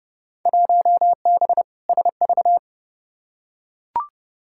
Morse code, heard as a keyed tone, about 30 wpm, 700 Hz. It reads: 1B SV
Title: Field Day Exchanges Encore - 30wpm